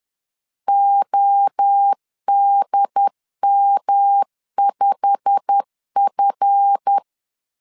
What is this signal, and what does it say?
Morse code, heard as a keyed tone, about 10 wpm, 800 Hz. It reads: ODM5F
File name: LEZIONI / 8 YJ45 / 8 YJ45 10wpm 2549